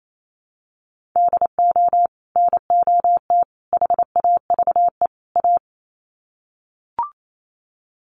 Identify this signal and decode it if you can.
Morse code, heard as a keyed tone, about 28 wpm, 700 Hz. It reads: DO NOT HAVE A
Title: Sets of 4 Words 28wpm